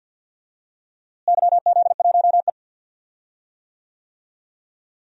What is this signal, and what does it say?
Morse code, heard as a keyed tone, about 45 wpm, 700 Hz. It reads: YZ1E
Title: DXCC-100 Call Signs - 4 Characters - Repeated 3x - 50wpm